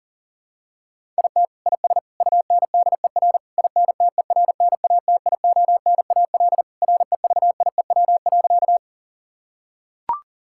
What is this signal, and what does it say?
Morse code, heard as a keyed tone, about 40 wpm, 700 Hz. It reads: IT IS UNDER INTERNATIONAL REVIEW.
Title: Sentences from Top 1000 Words - Review Part 6 - 40wpm